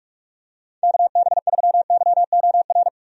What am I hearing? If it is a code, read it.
Morse code, heard as a keyed tone, about 41 wpm, 700 Hz. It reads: KB3YOR
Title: USA Call Signs - 6 Characters - Repeated 2x - 45wpm